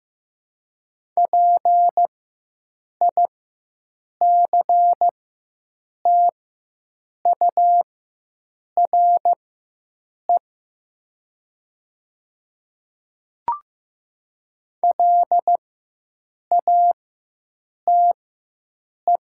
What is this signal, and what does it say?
Morse code, heard as a keyed tone, about 15 wpm, spaced as for 7 wpm, 700 Hz. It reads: PICTURE LATE
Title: 4x Character Spacing - Top 300 Words - 15wpm